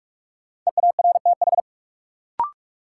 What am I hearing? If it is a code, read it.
Morse code, heard as a keyed tone, about 46 wpm, 700 Hz. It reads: EARTH
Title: Sets of 5 Words Encore - New Word Drill - 45wpm